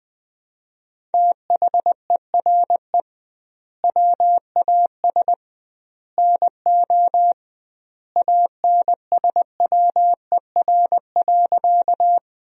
Morse code, heard as a keyed tone, about 21 wpm, 700 Hz. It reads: THERE WAS NO ANSWER.
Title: 2x Word-Spacing - Sentences from Top 300 Words - 20wpm